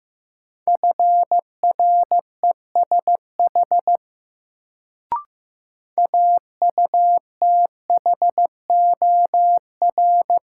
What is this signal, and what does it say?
Morse code, heard as a keyed tone, about 15 wpm, 700 Hz. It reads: FRESH AUTHOR
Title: Sentences from Top 1200 Words - New Word Drill - 15wpm